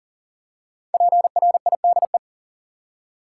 Morse code, heard as a keyed tone, about 40 wpm, 700 Hz. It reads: PRIDE
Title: Five Letter Words - 40wpm